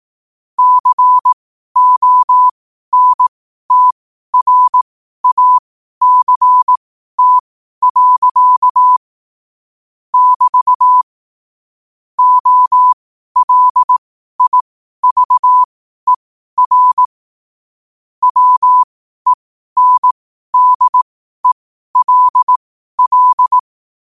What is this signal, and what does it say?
Morse code, heard as a keyed tone, about 18 wpm, spaced as for 12 wpm, 1000 Hz. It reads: CONTRACT. <BT> OLIVER WENDELL
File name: qotdfiles / 12wpm